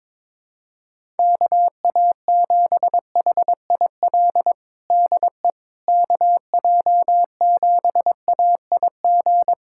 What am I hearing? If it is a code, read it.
Morse code, heard as a keyed tone, about 22 wpm, 700 Hz. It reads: KA7HIL DE KJ7AIG
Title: USA Callsign DE Callsign - 6 Characters - 22wpm